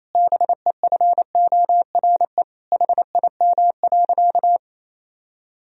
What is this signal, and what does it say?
Morse code, heard as a keyed tone, about 28 wpm, 700 Hz. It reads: BEFORE HIM.
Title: Sentences from Top 1000 Words - Review Part 2 - 28wpm